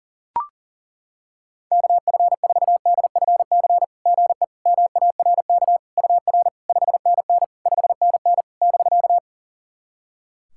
Morse code, heard as a keyed tone, about 40 wpm, 690 Hz. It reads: KF4DFC GE MARK UR 5NN 5NN <BK>
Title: SOTA QSO Practice 40wpm